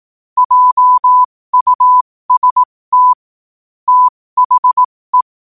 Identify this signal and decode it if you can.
Morse code, heard as a keyed tone, about 18 wpm, spaced as for 15 wpm, 1000 Hz. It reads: JUST THE
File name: qotdfiles / 15wpm